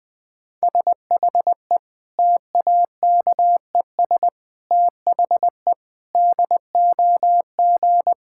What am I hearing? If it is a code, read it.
Morse code, heard as a keyed tone, about 20 wpm, 700 Hz. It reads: SHE TAKES THE DOG
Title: Difficult Sentences with Hints - 20wpm